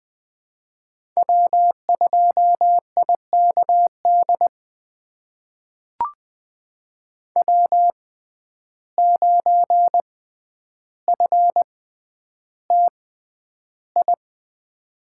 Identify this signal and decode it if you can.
Morse code, heard as a keyed tone, about 20 wpm, spaced as for 10 wpm, 700 Hz. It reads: W2IKD W9FTI
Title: USA Call Signs - 5 Characters - 6x Character Spacing - 20wpm